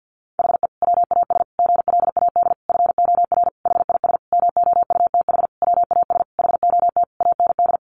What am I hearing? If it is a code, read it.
Morse code, heard as a keyed tone, about 50 wpm, 700 Hz. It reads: HE WAS GLAD FOR HIS MOUTH WAS HOT AND
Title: Sentences from Top 1500 Words 50wpm